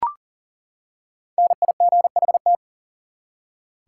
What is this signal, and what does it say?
Morse code, heard as a keyed tone, about 40 wpm, 700 Hz. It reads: NIGHT
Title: Five Letter Words - 40wpm